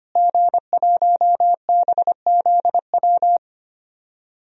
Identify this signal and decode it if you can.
Morse code, heard as a keyed tone, about 25 wpm, 700 Hz. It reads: G1BZW
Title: DXCC-100 Call Signs - 5 Characters - Repeated 3x - 25wpm